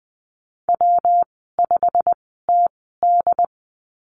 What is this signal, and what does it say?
Morse code, heard as a keyed tone, about 19 wpm, spaced as for 14 wpm, 700 Hz. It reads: W5TD
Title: 128 Lesson 21g - Number 6 - Call Signs 20wpm